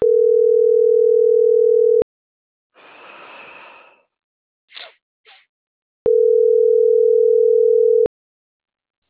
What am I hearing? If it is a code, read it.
Ringback tone (North America), 440 + 480 Hz.